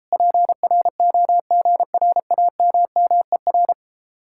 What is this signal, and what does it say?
Morse code, heard as a keyed tone, about 37 wpm, 700 Hz. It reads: PROGRAMMER